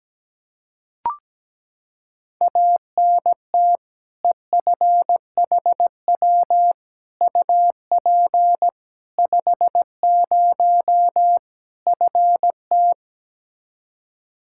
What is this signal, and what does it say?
Morse code, heard as a keyed tone, about 17 wpm, 700 Hz. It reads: ANT EFHW UP 50 FT
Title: Practice QSO - Exchange 2 - Info - 17wpm